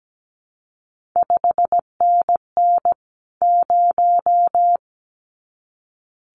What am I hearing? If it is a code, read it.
Morse code, heard as a keyed tone, about 17 wpm, 700 Hz. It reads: 5NN 0
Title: CQ WW DX Contest Exchange Encore - 17wpm